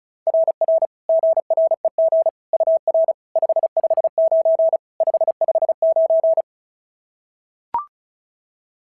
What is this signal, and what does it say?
Morse code, heard as a keyed tone, about 36 wpm, 690 Hz. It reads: RR GREG UR 559 559
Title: SOTA QSO Practice 35wpm